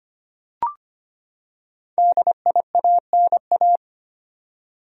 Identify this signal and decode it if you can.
Morse code, heard as a keyed tone, about 25 wpm, 700 Hz. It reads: DIANA
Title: USA Common Names - 25wpm